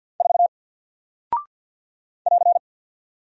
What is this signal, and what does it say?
Morse code, heard as a keyed tone, about 48 wpm, 700 Hz. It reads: 4 <AR>
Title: Single Character (Letter:Number:Punctuation:Prosign) Alt 50wpm